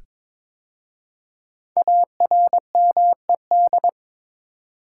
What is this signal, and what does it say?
Morse code, heard as a keyed tone, about 22 wpm, 690 Hz. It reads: ARMED
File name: 5 Letter Words - First 3 Letters Given - 22wpm